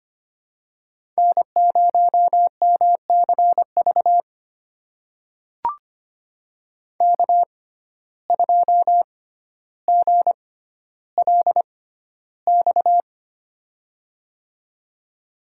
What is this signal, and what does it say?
Morse code, heard as a keyed tone, about 25 wpm, spaced as for 13 wpm, 700 Hz. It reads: N0MCV K2GLX
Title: USA Call Signs - 5 Characters - 6x Character Spacing - 25wpm